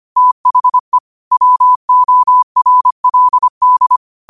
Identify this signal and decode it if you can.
Morse code, heard as a keyed tone, about 25 wpm, 1000 Hz. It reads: THE WORLD
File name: qotdfiles / 25wpm